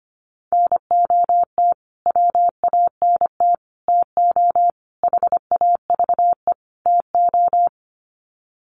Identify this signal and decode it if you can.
Morse code, heard as a keyed tone, about 25 wpm, 700 Hz. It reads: NOT WANT TO HAVE TO
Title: Sets of 5 Words Encore2 25wpm